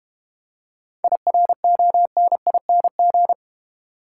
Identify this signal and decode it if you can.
Morse code, heard as a keyed tone, about 30 wpm, 690 Hz. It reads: IRONING
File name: Easy Words - Speed-Racing with Hints - 25wpm